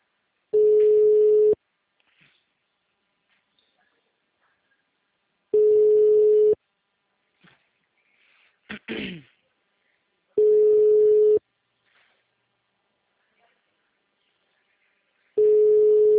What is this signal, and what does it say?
Ringback tone (Europe, 425 Hz).